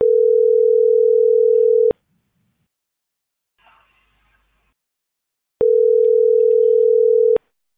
Ringback tone (North America), 440 + 480 Hz.